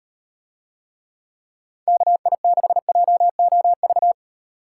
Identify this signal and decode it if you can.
Morse code, heard as a keyed tone, about 36 wpm, 700 Hz. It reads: KI6JOV